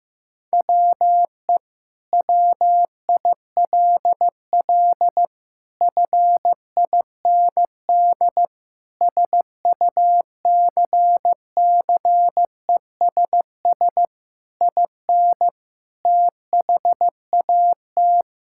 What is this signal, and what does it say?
Morse code, heard as a keyed tone, about 15 wpm, 700 Hz. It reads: WE WILL FIND SUCCESS IN THAT
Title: Sentences from Top 1000 Words - Review Part 5 - 15wpm